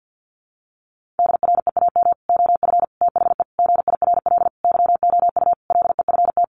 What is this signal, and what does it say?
Morse code, heard as a keyed tone, about 50 wpm, 700 Hz. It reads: DREAM OF THE GIRL YOU LEFT
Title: Sentences from Top 1500 Words 50wpm